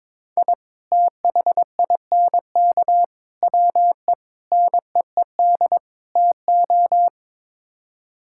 Morse code, heard as a keyed tone, about 22 wpm, 700 Hz. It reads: I THINK WE NEED TO
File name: practice / 22wpm / Build A Phrase - 5-Word Phrases - 22wpm